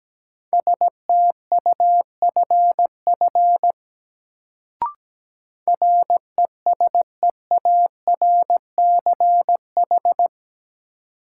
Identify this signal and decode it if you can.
Morse code, heard as a keyed tone, about 17 wpm, 700 Hz. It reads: STUFF RESEARCH
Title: Sentences from Top 1400 Words - New Word Drill Alt - 17wpm